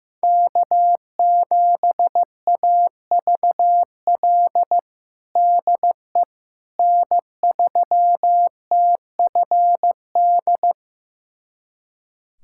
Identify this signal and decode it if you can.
Morse code, heard as a keyed tone, about 15 wpm, 700 Hz. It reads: K7AVL DE N3TFD
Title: USA Callsign DE Callsign - 5 Characters - 15wpm